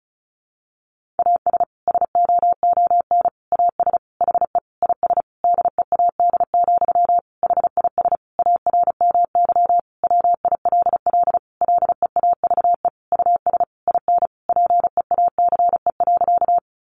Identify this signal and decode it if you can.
Morse code, heard as a keyed tone, about 34 wpm, 700 Hz. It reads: AS SOON AS HE IS DEAD, HIS ARMY WILL LEAVE US IN PEACE.
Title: Sentences from Top 1500 Words 35wpm